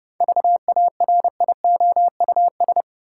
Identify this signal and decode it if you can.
Morse code, heard as a keyed tone, about 30 wpm, 700 Hz. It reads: VARIOUS
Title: Seven Letter Words - 30wpm